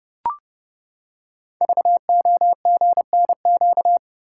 Morse code, heard as a keyed tone, about 31 wpm, 690 Hz. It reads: VOGNQ
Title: Code Groups - 5 Letters - 30wpm